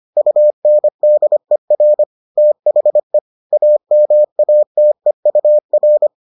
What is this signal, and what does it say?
Morse code, heard as a keyed tone, about 25 wpm, 600 Hz. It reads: UNDER THE AMATEUR